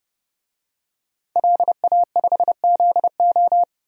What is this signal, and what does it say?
Morse code, heard as a keyed tone, about 31 wpm, 700 Hz. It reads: LA5ZO
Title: DXCC-100 Call Signs - 5 Characters - 30wpm